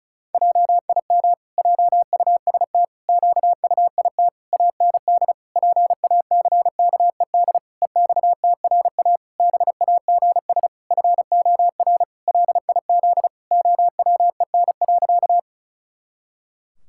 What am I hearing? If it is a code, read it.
Morse code, heard as a keyed tone, about 34 wpm, 700 Hz. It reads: JIM JUST QUIT AND PACKED EXTRA BAGS FOR LIZ OWEN.
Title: Pangram Challenge v2 - 35wpm